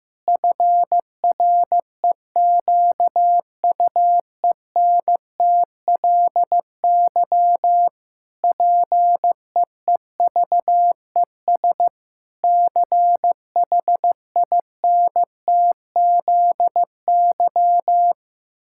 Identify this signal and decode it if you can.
Morse code, heard as a keyed tone, about 15 wpm, 700 Hz. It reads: FREQUENTLY PEEVES CHINTZY
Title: Pangram Challenge v2 - 15wpm